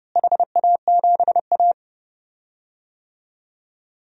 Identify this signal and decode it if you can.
Morse code, heard as a keyed tone, about 32 wpm, 700 Hz. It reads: HA7A
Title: DXCC-100 Call Signs - 4 Characters - Repeated 2x - 30wpm